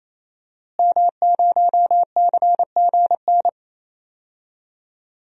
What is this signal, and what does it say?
Morse code, heard as a keyed tone, about 27 wpm, 700 Hz. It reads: M0CGN
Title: DXCC-100 Call Signs - 5 Characters - Repeated 3x - 28wpm